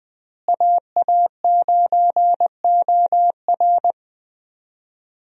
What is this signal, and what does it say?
Morse code, heard as a keyed tone, about 21 wpm, 700 Hz. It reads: AA9OR
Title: USA Call Signs - 5 Characters - Repeated 3x - 20wpm